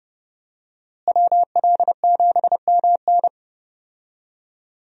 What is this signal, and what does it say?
Morse code, heard as a keyed tone, about 30 wpm, 700 Hz. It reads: WL7MN
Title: USA Call Signs - 5 Characters - Repeated 3x - 30wpm